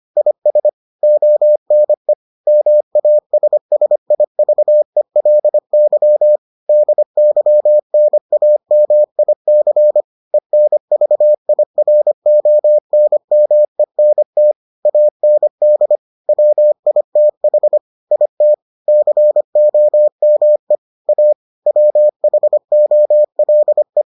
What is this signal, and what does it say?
Morse code, heard as a keyed tone, about 25 wpm, 600 Hz. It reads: IS ONE MASSIVELY DYNAMIC ENVIRONMENT AND WITH IT COME A WHOLE